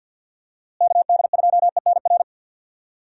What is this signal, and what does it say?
Morse code, heard as a keyed tone, about 51 wpm, 700 Hz. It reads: KD2ENR
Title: USA Call Signs - 6 Characters - Repeated 2x - 50wpm